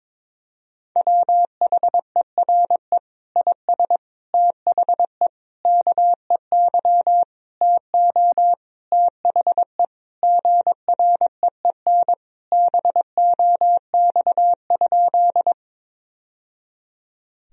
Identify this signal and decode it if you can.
Morse code, heard as a keyed tone, about 22 wpm, 700 Hz. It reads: WHERE IS THE KEY TO THE GREEN BOX?
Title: Sentences from Top 1000 Words - Review Part 4 - 22wpm